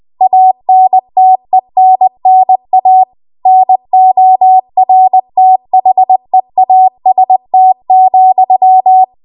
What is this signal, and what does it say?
Morse code, heard as a keyed tone, about 20 wpm, 750 Hz. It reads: ANTENNA NORTHEAST,
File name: Archive / 20 WPM / 20WPM